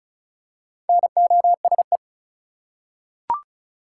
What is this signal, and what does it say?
Morse code, heard as a keyed tone, about 36 wpm, 700 Hz. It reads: NOSE
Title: Four Letter Words - 35wpm